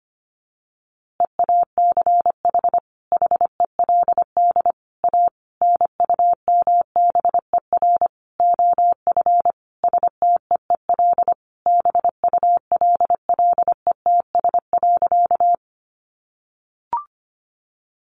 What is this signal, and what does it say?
Morse code, heard as a keyed tone, about 25 wpm, 700 Hz. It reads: EACH HELD A NUMBER OF STEEL BULLETS.
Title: A Sound Of Thunder 25wpm